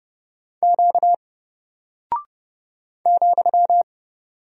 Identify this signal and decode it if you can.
Morse code, heard as a keyed tone, about 30 wpm, 700 Hz. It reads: Q ,
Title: Single Character (Letter:Number:Punctuation:Prosign) 30wpm